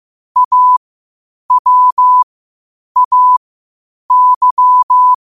Morse code, heard as a keyed tone, about 15 wpm, spaced as for 8 wpm, 1000 Hz. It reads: AWAY
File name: qotdfiles / 8wpm